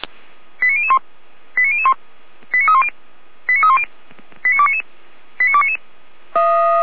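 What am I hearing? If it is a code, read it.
Five-tone selective call (ZVEI), heard as quick runs of five tones, tones of about 70 ms. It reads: 89001 89001 89310 89310 89299 89299
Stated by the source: ZVEI five-tone selective call 89350 received by SDR.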